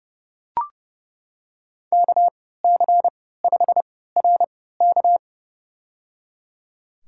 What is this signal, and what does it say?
Morse code, heard as a keyed tone, about 30 wpm, spaced as for 17 wpm, 700 Hz. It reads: KC5RK
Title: USA Call Signs - 5 Characters - 3x Character Spacing - 30wpm